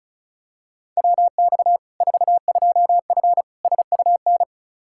Morse code, heard as a keyed tone, about 36 wpm, 700 Hz. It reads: WX 42F SUN
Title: K1USN SST QSO Practice 35wpm